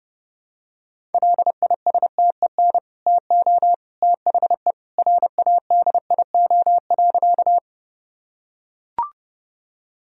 Morse code, heard as a keyed tone, about 30 wpm, 700 Hz. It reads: LISTEN TO THE RADIO.
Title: Sentences from Top 1000 Words - Review Part 3 - 30wpm